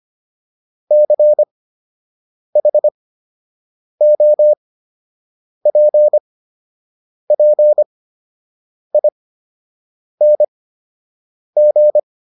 Morse code, heard as a keyed tone, about 25 wpm, spaced as for 7 wpm, 600 Hz. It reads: CHOPPING